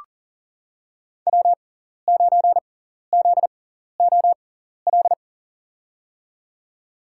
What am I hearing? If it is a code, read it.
Morse code, heard as a keyed tone, about 40 wpm, spaced as for 14 wpm, 700 Hz. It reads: W9ZOL
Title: USA Call Signs - 5 Characters - 6x Character Spacing - 40wpm